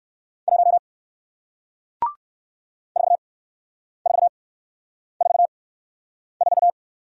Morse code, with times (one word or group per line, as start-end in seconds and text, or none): 0.5-0.8 1
3.0-6.7 4444